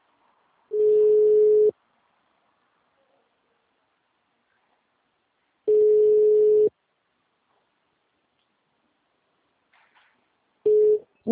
Ringback tone (Europe, 425 Hz).